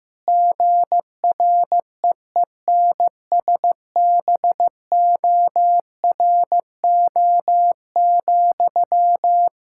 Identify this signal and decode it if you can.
Morse code, heard as a keyed tone, about 15 wpm, 700 Hz. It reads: GREENSBORO,
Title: K1USN SST QSO Practice 15wpm